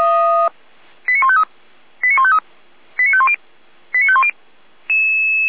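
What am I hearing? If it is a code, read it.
Five-tone selective call (ZVEI), heard as quick runs of five tones, tones of about 70 ms. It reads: 89153 89153 89510 89510
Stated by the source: Five-tone ZVEI selective call received by SDR, fire brigade Feuerwehr Orlamünde.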